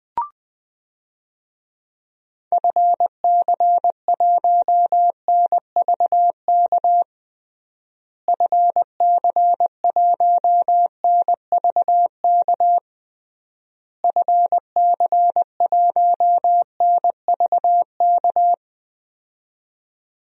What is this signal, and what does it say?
Morse code, heard as a keyed tone, about 19 wpm, 700 Hz. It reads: FC1NVK FC1NVK FC1NVK
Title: DXCC-100 Call Signs - 6 Characters - Repeated 3x - 20wpm